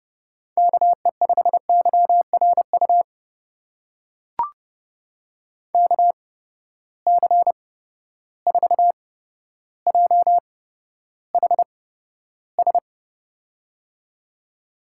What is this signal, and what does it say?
Morse code, heard as a keyed tone, about 30 wpm, spaced as for 13 wpm, 700 Hz. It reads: KE5YRU KC4JHS
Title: USA Call Signs - 6 Characters - 8x Character Spacing - 30wpm